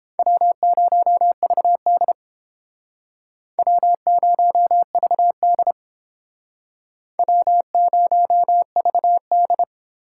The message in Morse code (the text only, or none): W0VD W0VD W0VD